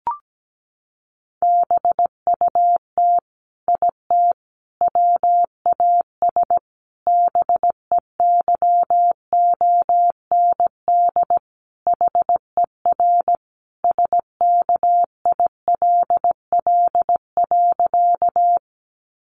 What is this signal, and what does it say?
Morse code, heard as a keyed tone, about 17 wpm, 700 Hz. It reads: BUT IT WAS BEYOND HER SKILL.
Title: Sentences from Top 1200 Words 17wpm